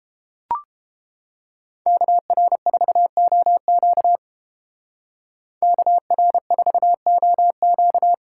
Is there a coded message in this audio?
Yes: Morse code from 1.9 to 8.1 s, heard as a keyed tone, about 30 wpm, 700 Hz. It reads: KR4OQ KR4OQ